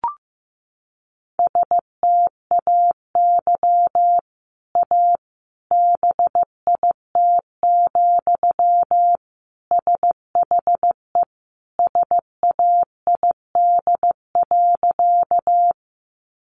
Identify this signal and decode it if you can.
Morse code, heard as a keyed tone, about 15 wpm, 700 Hz. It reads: STAY A BIT, SHE SAID.
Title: Sentences from Top 900 Words 15wpm